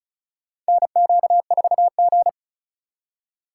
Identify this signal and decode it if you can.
Morse code, heard as a keyed tone, about 36 wpm, 700 Hz. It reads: NQ4G
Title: USA Call Signs - 4 Characters - 3x Character Spacing - 35wpm